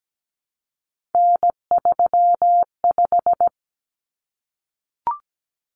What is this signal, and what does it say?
Morse code, heard as a keyed tone, about 17 wpm, 700 Hz. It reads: N35
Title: Common Cut Numbers - 3-Digit 17wpm